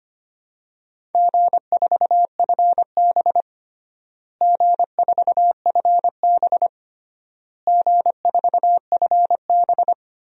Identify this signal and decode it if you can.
Morse code, heard as a keyed tone, about 25 wpm, 700 Hz. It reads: G4FB G4FB G4FB
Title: DXCC-100 Call Signs - 4 Characters - Repeated 3x - 25wpm